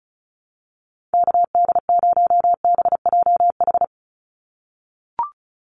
Morse code, heard as a keyed tone, about 34 wpm, 700 Hz. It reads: KD0BJH
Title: USA Call Signs 35wpm